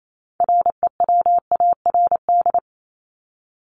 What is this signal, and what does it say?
Morse code, heard as a keyed tone, about 28 wpm, 700 Hz. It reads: REWARD
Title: Top 1751-2000 Words 28wpm